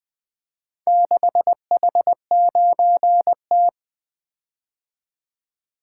Morse code, heard as a keyed tone, about 20 wpm, 700 Hz. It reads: 6H9T
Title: Four Letter-Number Combinations - 20wpm